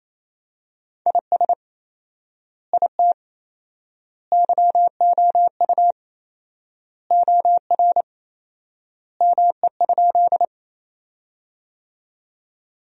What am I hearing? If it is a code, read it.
Morse code, heard as a keyed tone, about 28 wpm, 700 Hz. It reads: IS IT YOU OR ME?
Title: 4x Word-Spacing - Sentences from Top 200 Words - 28wpm